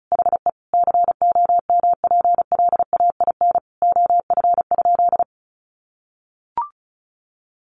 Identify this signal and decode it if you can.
Morse code, heard as a keyed tone, about 34 wpm, 700 Hz. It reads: HE COMPLAIN OF?
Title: Sentences from Top 1300 Words 35wpm